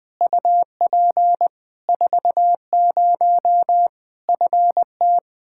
Morse code, heard as a keyed tone, about 20 wpm, 700 Hz. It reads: UP 40 FT
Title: Practice QSO - Exchange 2 - Info - 20wpm